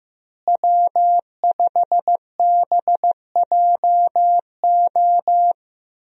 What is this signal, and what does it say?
Morse code, heard as a keyed tone, about 15 wpm, 700 Hz. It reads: W5BJO
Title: USA Call Signs - 5 Characters - Repeated 3x - 15wpm